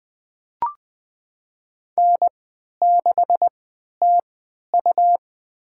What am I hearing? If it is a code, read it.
Morse code, heard as a keyed tone, about 20 wpm, spaced as for 11 wpm, 700 Hz. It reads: N6TU
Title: USA Call Signs - 4 Characters - 3x Character Spacing - 20wpm